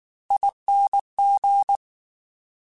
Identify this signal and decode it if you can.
Morse code, heard as a keyed tone, about 19 wpm, 800 Hz. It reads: ING
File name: Special INGs / Special INGs-mp3s / Slow ing @18